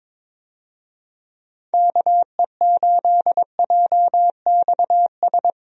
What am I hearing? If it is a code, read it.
Morse code, heard as a keyed tone, about 22 wpm, 700 Hz. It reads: KE8JXS
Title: USA Call Signs - 6 Characters - 22wpm